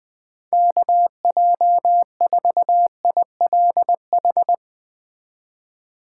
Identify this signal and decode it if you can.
Morse code, heard as a keyed tone, about 19 wpm, 700 Hz. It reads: KJ4ILH
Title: Speed Racing - USA 2x3 Call Signs 17wpm